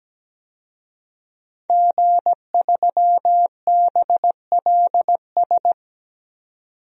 Morse code, heard as a keyed tone, about 17 wpm, 700 Hz. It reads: G3BLS
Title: DXCC-100 Call Signs - 5 Characters - 17wpm